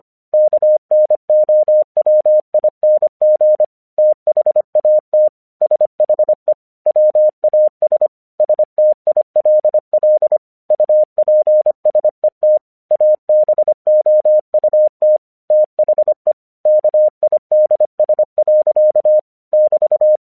Morse code, heard as a keyed tone, about 25 wpm, 610 Hz. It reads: KNOWING THAT SHE WAS STILL UPSET ABOUT THE KIDS. <BT>
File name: 1 English 25wpm / AQC19